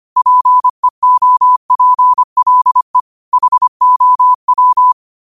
Morse code, heard as a keyed tone, about 25 wpm, 1000 Hz. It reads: PEOPLE HOW
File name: qotdfiles / 25wpm